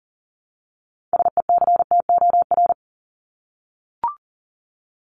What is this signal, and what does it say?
Morse code, heard as a keyed tone, about 40 wpm, 700 Hz. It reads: SECTOR
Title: Top 1751-2000 Words 40wpm